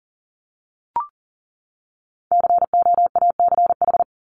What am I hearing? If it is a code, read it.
Morse code, heard as a keyed tone, about 40 wpm, 700 Hz. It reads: COACH
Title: Top 1751-2000 Words 40wpm